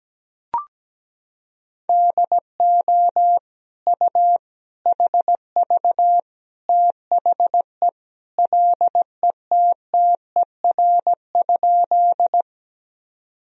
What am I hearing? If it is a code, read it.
Morse code, heard as a keyed tone, about 17 wpm, 700 Hz. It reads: DO U HV THE LETTER?
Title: Sentences from Top 300 Words and Abbreviations 17wpm